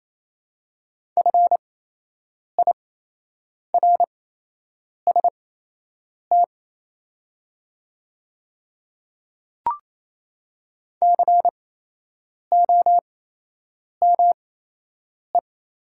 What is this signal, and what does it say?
Morse code, heard as a keyed tone, about 28 wpm, spaced as for 8 wpm, 700 Hz. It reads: FIRST COME
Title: 8x Character Spacing - Top 100 Words - 28wpm